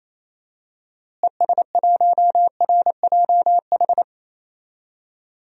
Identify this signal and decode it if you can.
Morse code, heard as a keyed tone, about 28 wpm, 700 Hz. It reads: ES1RJH
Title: DXCC-100 Call Signs - 6 Characters - Repeated 2x - 28wpm